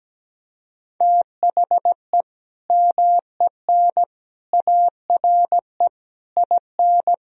Morse code, heard as a keyed tone, about 17 wpm, 700 Hz. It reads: THE MEN ARE IN
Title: Sentences from Top 1000 Words - Review Part 1 - 17wpm